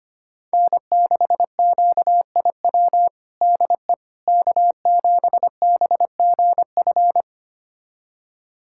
Morse code, heard as a keyed tone, about 25 wpm, 700 Hz. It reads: N6QIW DE K7BGF
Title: USA Callsign DE Callsign - 5 Characters - 25wpm